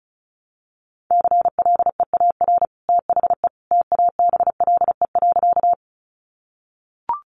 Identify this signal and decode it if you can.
Morse code, heard as a keyed tone, about 34 wpm, 700 Hz. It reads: CLEAR THE TABLE.
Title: Sentences from Top 500 Words 35wpm